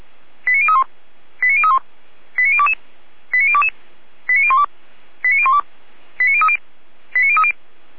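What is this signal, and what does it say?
Five-tone selective call (ZVEI), heard as quick runs of five tones, tones of about 70 ms. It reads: 89031 89031 89022 89022 89012 89012 89030 89030
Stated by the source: ZVEI five-tone selective call received by SDR.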